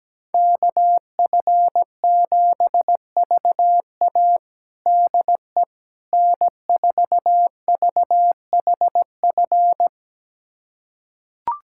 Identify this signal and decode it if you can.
Morse code, heard as a keyed tone, about 17 wpm, 700 Hz. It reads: KF7VA DE N4VHF
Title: USA Callsign DE Callsign - 5 Characters - 17wpm